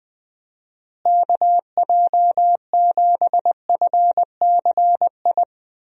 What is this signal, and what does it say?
Morse code, heard as a keyed tone, about 20 wpm, 700 Hz. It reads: KJ7FCI
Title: USA Call Signs - 6 Characters - Repeated 3x - 20wpm